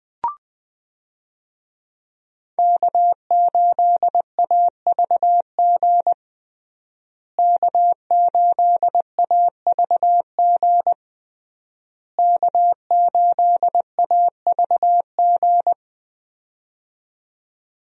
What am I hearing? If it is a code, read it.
Morse code, heard as a keyed tone, about 20 wpm, 700 Hz. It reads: K8AVG K8AVG K8AVG
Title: USA Call Signs - 5 Characters - Repeated 3x - 20wpm